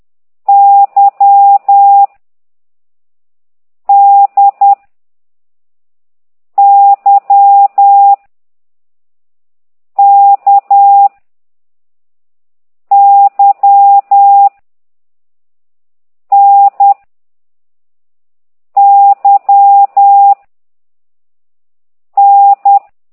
Morse code, heard as a keyed tone, about 10 wpm, 800 Hz. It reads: Y D Y K Y N Y N
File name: ct2idw / exerc4